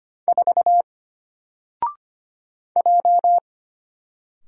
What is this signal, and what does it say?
Morse code, heard as a keyed tone, about 25 wpm, 700 Hz. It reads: 4 J
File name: Single-Letter-Number-25wpm